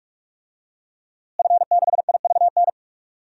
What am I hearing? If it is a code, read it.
Morse code, heard as a keyed tone, about 44 wpm, 700 Hz. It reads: F6IVN
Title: DXCC-100 Call Signs - 5 Characters - Repeated 3x - 45wpm